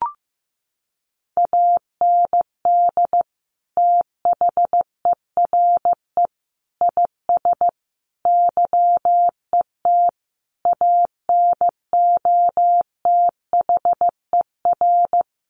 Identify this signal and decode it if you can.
Morse code, heard as a keyed tone, about 15 wpm, 700 Hz. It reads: AND THERE IS YET ANOTHER
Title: Sentences from Top 1500 Words 15wpm